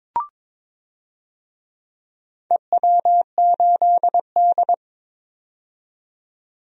Morse code, heard as a keyed tone, about 22 wpm, 700 Hz. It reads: EW8D
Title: DXCC-100 Call Signs - 4 Characters - 22wpm